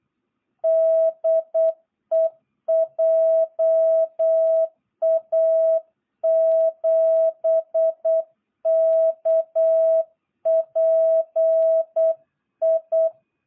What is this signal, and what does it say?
Morse code, heard as a keyed tone, about 8 wpm, 660 Hz. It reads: DE JA7KPI
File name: jh8cbh / 2022cw